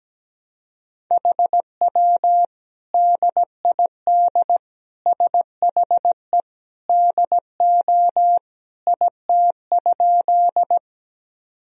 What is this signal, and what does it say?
Morse code, heard as a keyed tone, about 17 wpm, 700 Hz. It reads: HW DID SHE DO IT?
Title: Sentences from Top 100 Words and Abbreviations 17wpm